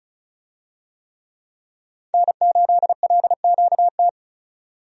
Morse code, heard as a keyed tone, about 36 wpm, 700 Hz. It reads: N8LQT
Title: USA Call Signs - 5 Characters - 35wpm